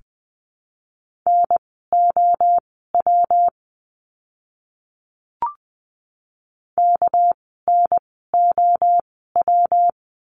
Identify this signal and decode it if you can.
Morse code, heard as a keyed tone, about 20 wpm, spaced as for 15 wpm, 700 Hz. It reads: NOW KNOW
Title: 230 Lesson 37f - Letter x - Words 20wpm